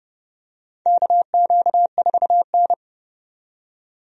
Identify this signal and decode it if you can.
Morse code, heard as a keyed tone, about 30 wpm, 700 Hz. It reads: KQ4N
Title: USA Call Signs - 4 Characters - 30wpm